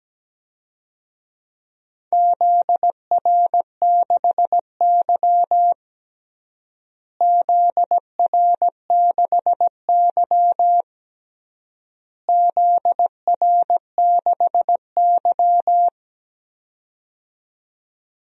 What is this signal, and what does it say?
Morse code, heard as a keyed tone, about 17 wpm, 700 Hz. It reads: ZR6Y ZR6Y ZR6Y
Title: DXCC-100 Call Signs - 4 Characters - Repeated 3x - 17wpm